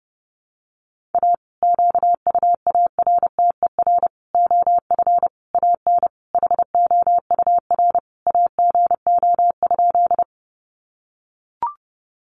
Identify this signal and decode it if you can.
Morse code, heard as a keyed tone, about 30 wpm, 700 Hz. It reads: A QUARTER OF AN HOUR AGO?
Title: Sentences from Top 1300 Words 30wpm